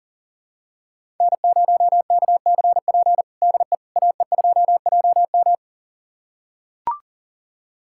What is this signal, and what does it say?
Morse code, heard as a keyed tone, about 40 wpm, 700 Hz. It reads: N0KCP DE AE2JM
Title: USA Callsign DE Callsign 40wpm